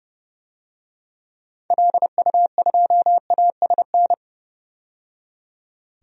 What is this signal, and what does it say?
Morse code, heard as a keyed tone, about 30 wpm, 700 Hz. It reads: LU2ASN
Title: DXCC-100 Call Signs - 6 Characters - 30wpm